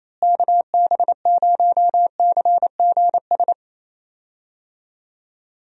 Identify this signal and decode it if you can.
Morse code, heard as a keyed tone, about 28 wpm, 700 Hz. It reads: KB0CGS
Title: USA Call Signs - 6 Characters - Repeated 2x - 28wpm